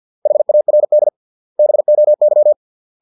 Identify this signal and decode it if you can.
Morse code, heard as a keyed tone, about 50 wpm, 600 Hz. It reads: HARD BOY